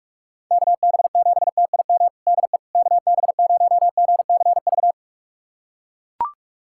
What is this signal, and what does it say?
Morse code, heard as a keyed tone, about 41 wpm, 700 Hz. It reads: KB7TIM DE KB0GCV
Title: USA Callsign DE Callsign - 6 Characters - 45wpm